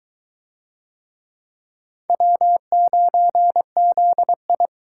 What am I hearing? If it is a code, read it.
Morse code, heard as a keyed tone, about 22 wpm, 700 Hz. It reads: W9ZI